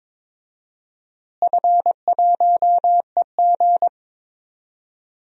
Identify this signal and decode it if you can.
Morse code, heard as a keyed tone, about 22 wpm, 700 Hz. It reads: F1EG
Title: DXCC-100 Call Signs - 4 Characters - Repeated 2x - 22wpm